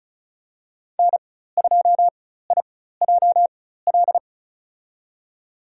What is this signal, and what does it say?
Morse code, heard as a keyed tone, about 36 wpm, spaced as for 16 wpm, 700 Hz. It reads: N2IJL
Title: USA Call Signs - 5 Characters - 4x Character Spacing - 35wpm